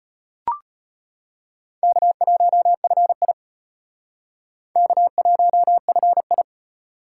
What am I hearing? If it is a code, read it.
Morse code, heard as a keyed tone, about 32 wpm, 700 Hz. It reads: K1FI K1FI